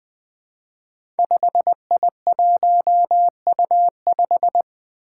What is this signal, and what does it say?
Morse code, heard as a keyed tone, about 20 wpm, 700 Hz. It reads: 5I1U5
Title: Five Challenging Letter-Number Combinations 20wpm